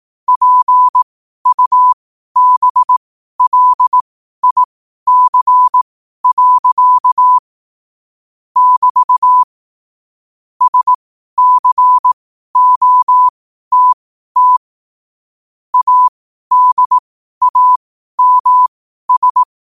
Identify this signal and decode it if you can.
Morse code, heard as a keyed tone, about 18 wpm, spaced as for 12 wpm, 1000 Hz. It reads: PUBLIC. <BT> SCOTT ADAMS
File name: qotdfiles / 12wpm